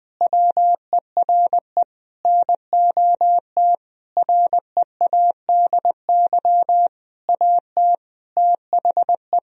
Morse code, heard as a keyed tone, about 20 wpm, 700 Hz. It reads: WERE NOT READY AT THE
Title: Sentences from Top 1000 Words - Review Part 2 - 20wpm